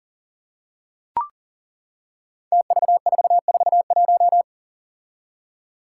Morse code, heard as a keyed tone, about 40 wpm, 700 Hz. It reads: TV441
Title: Five Challenging Letter-Number Combinations 40wpm